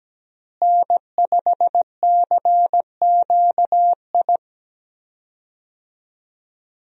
Morse code, heard as a keyed tone, about 17 wpm, 700 Hz. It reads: N5CQI
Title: USA Call Signs - 5 Characters - Repeated 3x - 17wpm